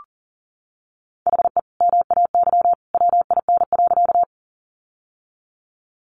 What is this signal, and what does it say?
Morse code, heard as a keyed tone, about 40 wpm, 700 Hz. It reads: HE MAY WIN.
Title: Sentences from Top 1500 Words 40wpm